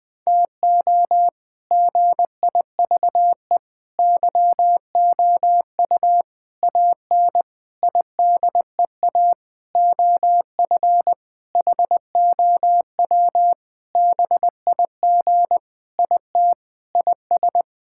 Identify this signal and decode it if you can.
Morse code, heard as a keyed tone, about 20 wpm, 700 Hz. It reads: TO GIVE YOU AN IDEA OF HOW BIG IT IS
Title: Sentences with 7-Word Phrases - 20wpm